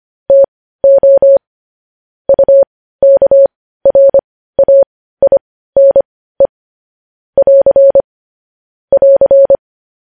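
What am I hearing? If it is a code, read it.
Morse code, heard as a keyed tone, about 25 wpm, spaced as for 15 wpm, 550 Hz. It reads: TO UKRAINE <AR> <AR>
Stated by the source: Morse code news headlines at 15 wpm.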